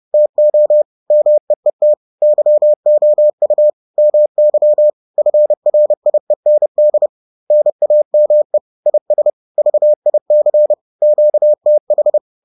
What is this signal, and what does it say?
Morse code, heard as a keyed tone, about 30 wpm, 600 Hz. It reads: TO MEET YOU MY FRIEND NAME IS VIC QTH